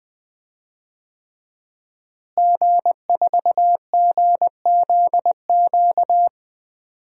Morse code, heard as a keyed tone, about 20 wpm, 700 Hz. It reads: G4GZQ
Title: DXCC-100 Call Signs - 5 Characters - Repeated 3x - 20wpm